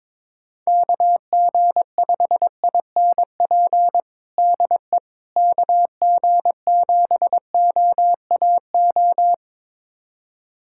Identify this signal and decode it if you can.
Morse code, heard as a keyed tone, about 22 wpm, 700 Hz. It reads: KG5INP DE KG7OAO
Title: USA Callsign DE Callsign - 6 Characters - 22wpm